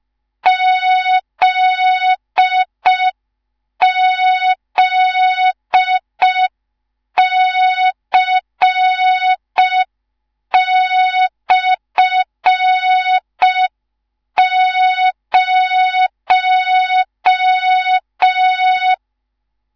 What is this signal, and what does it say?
Morse code, heard as a keyed tone, about 5 wpm, 3700 Hz. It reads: ZZC/0
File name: Morze-HA2MN / hang / 19 GYAKORLAT